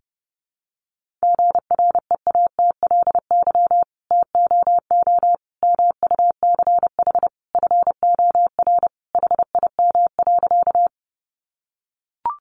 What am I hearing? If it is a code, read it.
Morse code, heard as a keyed tone, about 30 wpm, 700 Hz. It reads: GREATLY TOO MUCH FOR HIM.
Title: Sentences from Top 1400 Words 30wpm